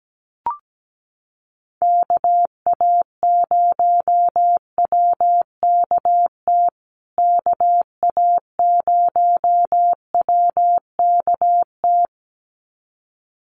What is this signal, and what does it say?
Morse code, heard as a keyed tone, about 17 wpm, 700 Hz. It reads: KA0WKT KA0WKT
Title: USA Call Signs Repeated 2x 17wpm